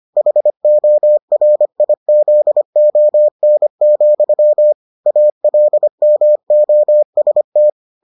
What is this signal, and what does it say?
Morse code, heard as a keyed tone, about 25 wpm, 600 Hz. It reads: HORIZON, ALMOST